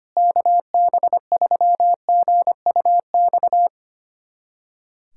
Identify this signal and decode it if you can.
Morse code, heard as a keyed tone, about 25 wpm, 700 Hz. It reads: KB3GUX
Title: USA Callsign DE Callsign - 6 Characters - 25wpm